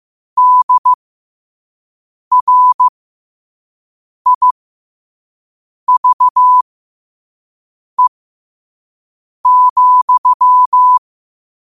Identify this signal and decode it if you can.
Morse code, heard as a keyed tone, about 15 wpm, spaced as for 5 wpm, 1000 Hz. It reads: DRIVE,
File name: qotdfiles / 5wpm